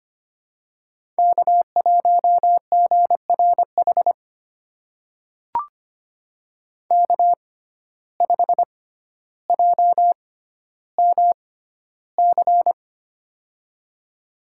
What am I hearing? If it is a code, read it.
Morse code, heard as a keyed tone, about 25 wpm, spaced as for 13 wpm, 700 Hz. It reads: K1GRH K5JMC
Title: USA Call Signs - 6x Character Spacing - 25wpm